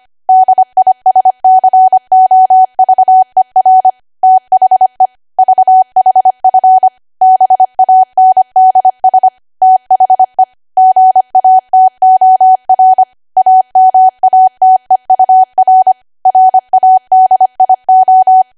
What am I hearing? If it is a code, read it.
Morse code, heard as a keyed tone, about 25 wpm, 750 Hz. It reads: DISCOVER THE VHF BANDS THE GATOR AMATEUR RADIO
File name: Archive / 25 WPM / 25WPM